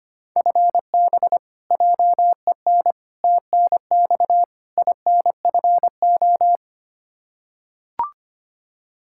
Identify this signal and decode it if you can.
Morse code, heard as a keyed tone, about 25 wpm, 700 Hz. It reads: FB JEN TNX INFO
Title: Practice QSO - Exchange 3 - More Info - 25wpm